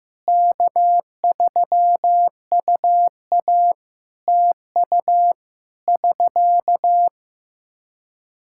Morse code, heard as a keyed tone, about 15 wpm, 700 Hz. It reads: K3UA TU <SK>
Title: Practice QSO - Exchange 4 - Ending - 15wpm